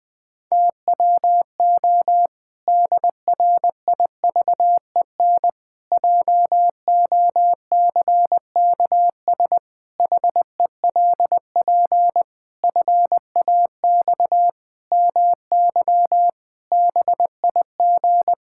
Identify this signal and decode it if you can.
Morse code, heard as a keyed tone, about 20 wpm, 700 Hz. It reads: TWO DRIVEN JOCKS HELP FAX MY BIG
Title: Pangram Challenge v2 - 20wpm